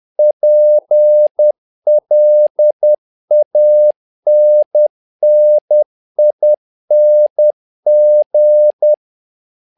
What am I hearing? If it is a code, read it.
Morse code, heard as a keyed tone, about 10 wpm, 600 Hz. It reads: PLANNING
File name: learning center / tower trouble / tower trouble 10wpm